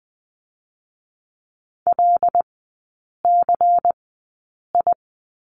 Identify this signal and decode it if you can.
Morse code, heard as a keyed tone, about 19 wpm, spaced as for 8 wpm, 690 Hz. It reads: LCI
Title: 093 Lesson 16e - Letter c - Three Characters 20wpm